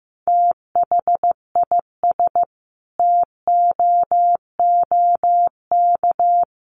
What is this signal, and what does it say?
Morse code, heard as a keyed tone, about 15 wpm, 700 Hz. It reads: THIS TOOK
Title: Sentences from Top 300 Words 15wpm